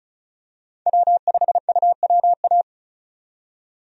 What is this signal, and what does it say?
Morse code, heard as a keyed tone, about 34 wpm, 700 Hz. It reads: W5UWA
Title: USA Call Signs - 5 Characters - 35wpm